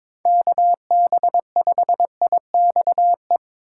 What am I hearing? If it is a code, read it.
Morse code, heard as a keyed tone, about 21 wpm, 700 Hz. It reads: KB5IXE
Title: USA Call Signs - 6 Characters - 3x Character Spacing - 22wpm